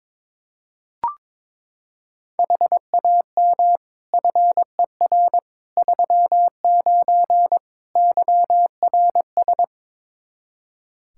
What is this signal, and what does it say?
Morse code, heard as a keyed tone, about 22 wpm, 700 Hz. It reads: HAM FER 39 YRS
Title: Practice QSO - Exchange 3 - More Info - 22wpm